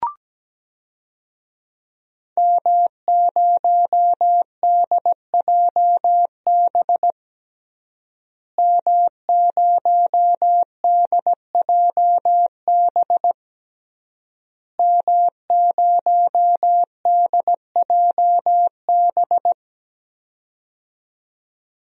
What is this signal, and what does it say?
Morse code, heard as a keyed tone, about 17 wpm, 700 Hz. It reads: M0DJB M0DJB M0DJB
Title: DXCC-100 Call Signs - 5 Characters - Repeated 3x - 17wpm